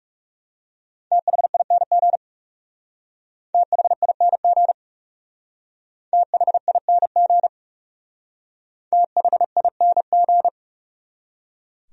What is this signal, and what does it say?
Morse code, heard as a keyed tone, about 36 wpm, 700 Hz. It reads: THING THING THING THING
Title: Speed-Racing Top 100 Words 30wpm